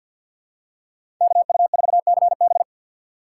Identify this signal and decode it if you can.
Morse code, heard as a keyed tone, about 46 wpm, 700 Hz. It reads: KU4CB
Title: USA Call Signs - 5 Characters - 2x Character Spacing - 50wpm